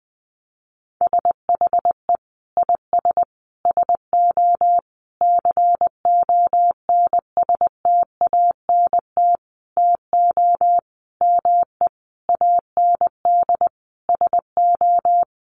Morse code, heard as a keyed tone, about 20 wpm, 700 Hz. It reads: SHE IS SO CONSTANT TO ME AND SO
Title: Sentences from Top 1500 Words 20wpm